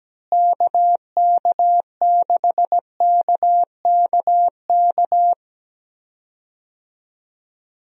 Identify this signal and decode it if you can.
Morse code, heard as a keyed tone, about 17 wpm, 700 Hz. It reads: KK6KKK
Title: USA Call Signs - 6 Characters - Repeated 3x - 17wpm